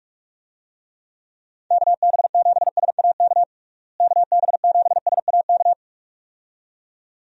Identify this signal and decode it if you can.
Morse code, heard as a keyed tone, about 46 wpm, 700 Hz. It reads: KB7SAK KB7SAK
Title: USA Call Signs - 6 Characters - Repeated 2x - 45wpm